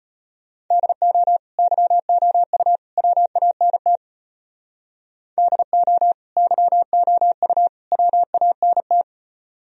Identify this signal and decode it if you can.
Morse code, heard as a keyed tone, about 32 wpm, 700 Hz. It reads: DO YOU WANT DO YOU WANT